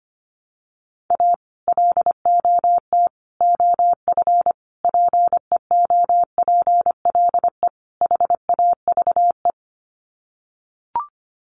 Morse code, heard as a keyed tone, about 25 wpm, 700 Hz. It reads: A LOT OF PEOPLE HAVE
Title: Sets of 5 Words 25wpm